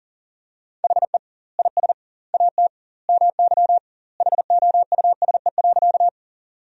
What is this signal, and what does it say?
Morse code, heard as a keyed tone, about 40 wpm, 700 Hz. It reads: HE IS AT MY HOUSE.
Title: 2x Word-Spacing - Sentences from Top 200 Words - 40wpm